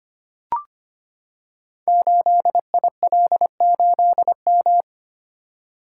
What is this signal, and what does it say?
Morse code, heard as a keyed tone, about 25 wpm, 700 Hz. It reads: 8IL8M
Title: Code Groups - 5 Letter:Numbers - 25wpm